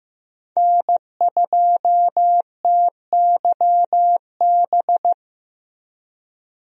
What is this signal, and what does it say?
Morse code, heard as a keyed tone, about 15 wpm, 700 Hz. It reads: N2TYB
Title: USA Call Signs - 5 Characters - Repeated 3x - 15wpm